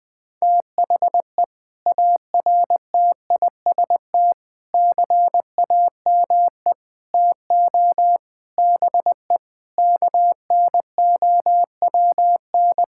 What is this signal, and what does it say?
Morse code, heard as a keyed tone, about 20 wpm, 700 Hz. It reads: THE ARTIST CAME TO BE KNOWN
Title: Sentences with 6-Word Phrases - 20wpm